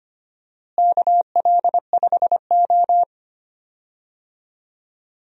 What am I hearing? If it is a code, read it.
Morse code, heard as a keyed tone, about 25 wpm, 700 Hz. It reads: KL5O
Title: USA Call Signs - 4 Characters - Repeated 3x - 25wpm